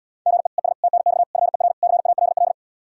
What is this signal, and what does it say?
Morse code, heard as a keyed tone, about 77 wpm, 700 Hz. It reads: GE ES TNX FER CALL